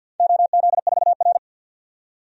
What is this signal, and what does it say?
Morse code, heard as a keyed tone, about 53 wpm, 700 Hz. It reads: OZ4R
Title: DXCC-100 Call Signs - 4 Characters - 50wpm